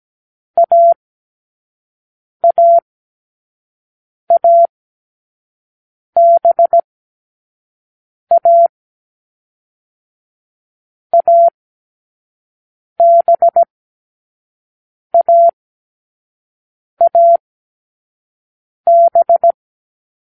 Morse code, heard as a keyed tone, about 17 wpm, spaced as for 5 wpm, 680 Hz. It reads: AAABA ABAAB